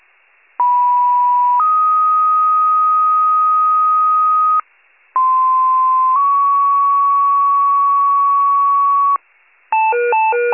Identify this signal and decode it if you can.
Two-tone page, heard as a one-second tone, then a three-second tone, about 980 then 1250 Hz, about 1040 then 1120 Hz. It is heard from 0.6 to 9.2 s.